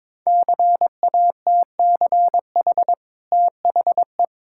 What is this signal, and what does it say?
Morse code, heard as a keyed tone, about 23 wpm, 690 Hz. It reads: CATCH THE
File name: practice / 22wpm / Easy Sentences with Hints - 22wpm